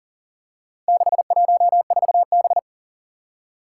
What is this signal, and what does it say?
Morse code, heard as a keyed tone, about 39 wpm, 700 Hz. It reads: 614B
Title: Four Challenging Letter-Number Combinations 40wpm